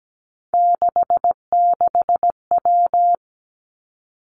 Morse code, heard as a keyed tone, about 17 wpm, 700 Hz. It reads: 66W
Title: Three Challenging Letter-Number Combinations 17wpm